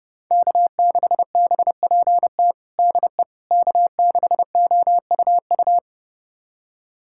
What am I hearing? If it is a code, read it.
Morse code, heard as a keyed tone, about 32 wpm, 700 Hz. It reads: K6BPT DE K6OUU
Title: USA Callsign DE Callsign - 5 Characters - 30wpm